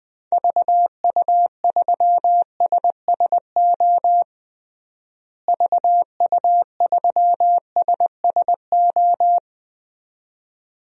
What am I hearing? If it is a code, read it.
Morse code, heard as a keyed tone, about 21 wpm, 700 Hz. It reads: VU3SSO VU3SSO
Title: DXCC-100 Call Signs - 6 Characters - Repeated 2x - 20wpm